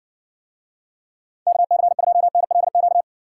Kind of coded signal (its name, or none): Morse code